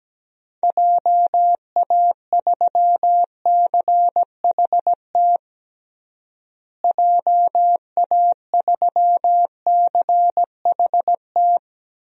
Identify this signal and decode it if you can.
Morse code, heard as a keyed tone, about 17 wpm, 700 Hz. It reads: JA3CHT JA3CHT
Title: DXCC-100 Call Signs - 6 Characters - Repeated 3x - 17wpm